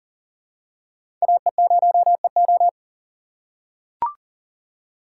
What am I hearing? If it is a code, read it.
Morse code, heard as a keyed tone, about 40 wpm, 700 Hz. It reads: AE0EO
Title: USA Call Signs - 5 Characters - 4x Character Spacing - 40wpm